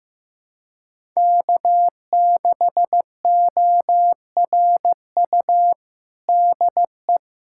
Morse code, heard as a keyed tone, about 15 wpm, 700 Hz. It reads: K6ORU DE
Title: USA Callsign DE Callsign - 5 Characters - 15wpm